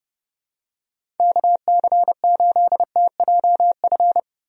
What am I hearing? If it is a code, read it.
Morse code, heard as a keyed tone, about 29 wpm, 700 Hz. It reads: KC8TJF
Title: USA Call Signs - 6 Characters - 30wpm